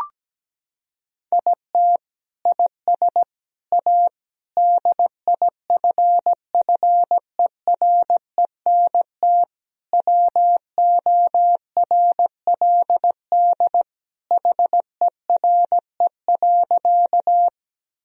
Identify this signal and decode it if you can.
Morse code, heard as a keyed tone, about 17 wpm, 700 Hz. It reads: IT IS A DIFFERENT WORLD HERE.
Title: Sentences from Top 1000 Words - Review Part 3 - 17wpm